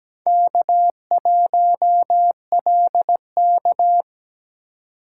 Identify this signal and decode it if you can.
Morse code, heard as a keyed tone, about 20 wpm, 700 Hz. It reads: K1LK
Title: Speed Racing - USA 1x2 Call Signs 17wpm